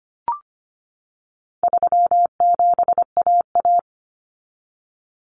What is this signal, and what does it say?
Morse code, heard as a keyed tone, about 25 wpm, 700 Hz. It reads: 37AA
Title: Common Cut Numbers - 4-Digit 25wpm